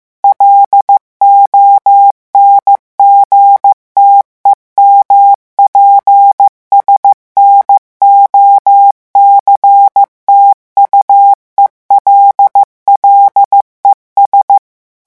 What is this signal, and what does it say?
Morse code, heard as a keyed tone, about 15 wpm, 800 Hz. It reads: LONGTEMPS NOCTUELLES
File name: morse mp3 / revision9 citations humoristiques vitesse 14 mots minute